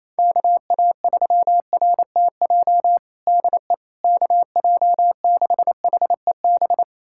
Morse code, heard as a keyed tone, about 28 wpm, 700 Hz. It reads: KA3RTJ DE KJ6HEB
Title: USA Callsign DE Callsign 28wpm